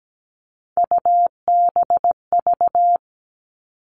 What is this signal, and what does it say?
Morse code, heard as a keyed tone, about 17 wpm, 700 Hz. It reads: UBV
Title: All Cut Numbers - 3-Digit 17wpm